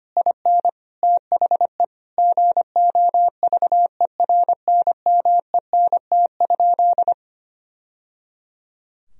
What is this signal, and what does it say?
Morse code, heard as a keyed tone, about 25 wpm, 700 Hz. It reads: IN THE GOVERNMENT?
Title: Sentences from Top 1000 Words - Review Part 4 - 25wpm